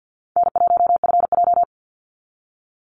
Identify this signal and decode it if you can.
Morse code, heard as a keyed tone, about 50 wpm, 700 Hz. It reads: N1FJ
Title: NA SOTA Activator Callsigns 50wpm